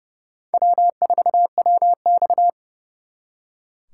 Morse code, heard as a keyed tone, about 37 wpm, 700 Hz. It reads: W4WX
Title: Speed Racing - USA 1x2 Call Signs 30wpm